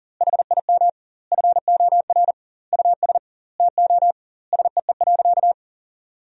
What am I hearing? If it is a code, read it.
Morse code, heard as a keyed tone, about 40 wpm, 700 Hz. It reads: HIM FOR US TO SEE.
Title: 2x Word-Spacing - Sentences from Top 500 Words - 40wpm